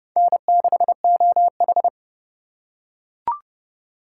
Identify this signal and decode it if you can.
Morse code, heard as a keyed tone, about 29 wpm, 700 Hz. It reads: N6OH